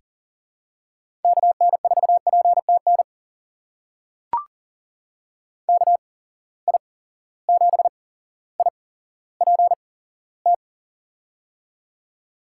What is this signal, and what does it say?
Morse code, heard as a keyed tone, about 40 wpm, spaced as for 17 wpm, 700 Hz. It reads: KN4PTN KI7IPT
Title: USA Call Signs - 6 Characters - 8x Character Spacing - 40wpm